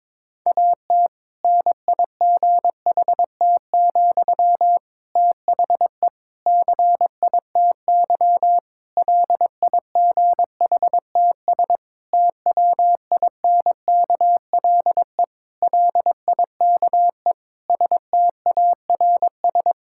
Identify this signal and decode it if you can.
Morse code, heard as a keyed tone, about 21 wpm, 700 Hz. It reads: AT NIGHT, THE CITY LIGHTS TWINKLE LIKE STARS
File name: Difficult Sentences with Hints - 22wpm